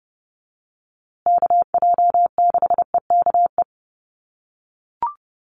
Morse code, heard as a keyed tone, about 30 wpm, 700 Hz. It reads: KJ6EKE
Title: USA Call Signs Repeated 2x 30wpm